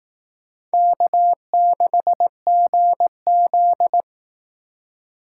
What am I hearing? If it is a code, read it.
Morse code, heard as a keyed tone, about 18 wpm, 700 Hz. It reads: K6GZ
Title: Speed Racing - USA Call Signs 15wpm